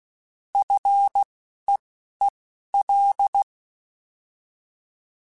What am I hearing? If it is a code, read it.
Morse code, heard as a keyed tone, about 16 wpm, spaced as for 11 wpm, 800 Hz. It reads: FEEL
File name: Combinations / Combinations-mp3s / Combinations@10